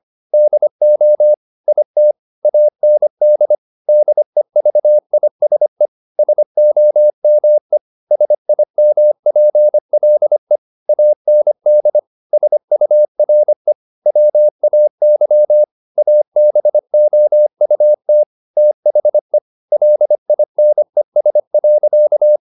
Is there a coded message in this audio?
Yes: Morse code from 0.3 to 22.4 s, heard as a keyed tone, about 25 wpm, 600 Hz. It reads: DO IT AND DEVISE SOME SIMPLE AND SURE WAY ABOUT THE LINES.